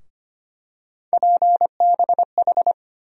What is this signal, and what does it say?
Morse code, heard as a keyed tone, about 25 wpm, 700 Hz. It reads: PBH
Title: USA Call Sign Postfixes - 3 Letters 25wpm